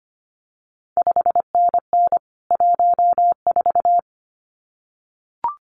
Morse code, heard as a keyed tone, about 25 wpm, 700 Hz. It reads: 5NN 14
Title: CQ WW DX Contest Exchange - 25wpm